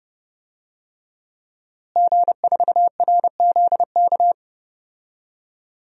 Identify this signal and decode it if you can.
Morse code, heard as a keyed tone, about 30 wpm, 700 Hz. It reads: G4RZK
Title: DXCC-100 Call Signs - 5 Characters - Repeated 3x - 30wpm